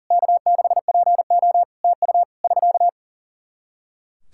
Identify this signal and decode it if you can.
Morse code, heard as a keyed tone, about 40 wpm, 700 Hz. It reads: K6PO TU <SK>
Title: Practice QSO - Exchange 4 - Ending - 40wpm